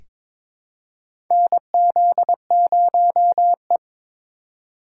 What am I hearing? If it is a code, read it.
Morse code, heard as a keyed tone, about 22 wpm, 700 Hz. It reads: NZ0E
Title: USA Call Signs - 4 Characters - 3x Character Spacing - 22wpm